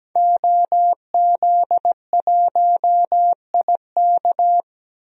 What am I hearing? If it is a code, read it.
Morse code, heard as a keyed tone, about 17 wpm, 700 Hz. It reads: OZ1IK
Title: DXCC-100 Call Signs - 5 Characters - Repeated 2x - 17wpm